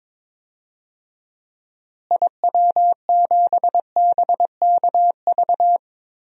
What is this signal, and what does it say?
Morse code, heard as a keyed tone, about 22 wpm, 700 Hz. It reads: IW7BKV
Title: DXCC-100 Call Signs - 6 Characters - Repeated 3x - 22wpm